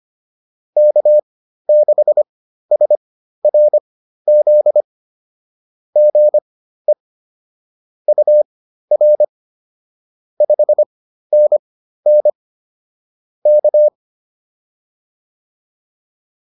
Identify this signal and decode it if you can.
Morse code, heard as a keyed tone, about 25 wpm, spaced as for 13 wpm, 600 Hz. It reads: K6SRZ GE UR 5NN K